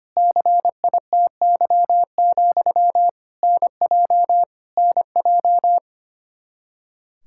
Morse code, heard as a keyed tone, about 25 wpm, 700 Hz. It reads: CITY, NJ NJ
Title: Practice QSO - Exchange 1 - Beginning - 25wpm